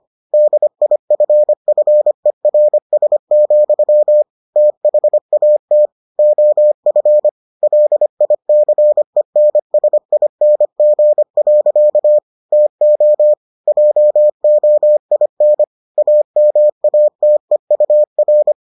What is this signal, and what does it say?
Morse code, heard as a keyed tone, about 25 wpm, 600 Hz. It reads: DIFFERS, THAT OF LICENSING. TO JOIN AMATEUR